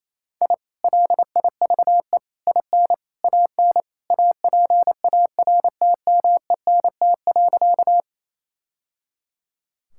Morse code, heard as a keyed tone, about 28 wpm, 700 Hz. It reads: I LIVE IN AN APARTMENT.
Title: Sentences from Top 1000 Words - Review Part 6 - 28wpm